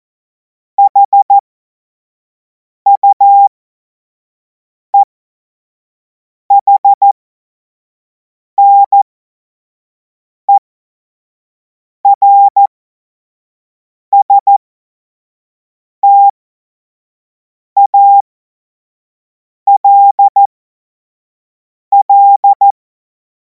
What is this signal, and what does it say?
Morse code, heard as a keyed tone, about 14 wpm, spaced as for 5 wpm, 800 Hz. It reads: HUEHNERSTALL
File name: Morsekurs / Witze / Witz 26. Mai 2026 25 F70